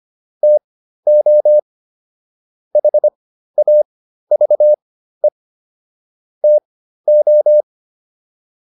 Morse code, heard as a keyed tone, about 25 wpm, spaced as for 13 wpm, 600 Hz. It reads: TO HAVE TO